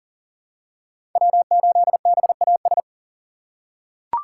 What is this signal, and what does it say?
Morse code, heard as a keyed tone, about 40 wpm, 700 Hz. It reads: W8BAS
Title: USA Call Signs - 5 Characters - 6x Character Spacing - 40wpm